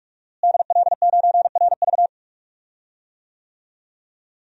Morse code, heard as a keyed tone, about 42 wpm, 700 Hz. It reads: DL9RV
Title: DXCC-100 Call Signs - 5 Characters - Repeated 2x - 45wpm